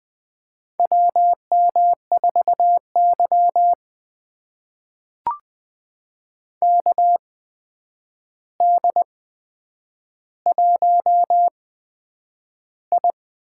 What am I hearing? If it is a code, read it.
Morse code, heard as a keyed tone, about 20 wpm, spaced as for 9 wpm, 700 Hz. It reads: WM4Y KD1I
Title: USA Call Signs - 4 Characters - 8x Character Spacing - 20wpm